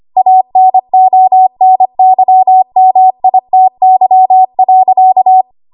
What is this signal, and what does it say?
Morse code, heard as a keyed tone, about 25 wpm, 750 Hz. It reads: ANONYMITY.
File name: Archive / 25 WPM / 25WPM